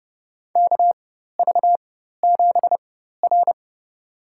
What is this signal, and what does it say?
Morse code, heard as a keyed tone, about 30 wpm, spaced as for 14 wpm, 700 Hz. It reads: KV7R
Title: USA Call Signs - 4 Characters - 4x Character Spacing - 30wpm